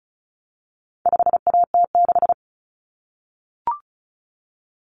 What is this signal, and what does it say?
Morse code, heard as a keyed tone, about 34 wpm, 700 Hz. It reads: 5AT6
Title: Common Cut Numbers - 4-Digit 35wpm